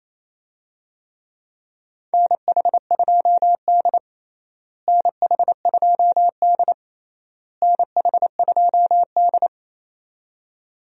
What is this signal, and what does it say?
Morse code, heard as a keyed tone, about 28 wpm, 700 Hz. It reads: NH2D NH2D NH2D
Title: USA Call Signs - 4 Characters - Repeated 3x - 28wpm